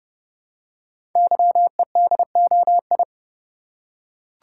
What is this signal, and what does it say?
Morse code, heard as a keyed tone, about 30 wpm, 700 Hz. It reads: YEDOI
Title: Code Groups - 5 Letters - 30wpm